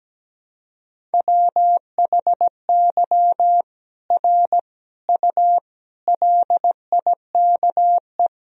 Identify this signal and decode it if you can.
Morse code, heard as a keyed tone, about 17 wpm, 700 Hz. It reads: WHY R U LIKE
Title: Sentences from Top 200 Words and Abbreviations 17wpm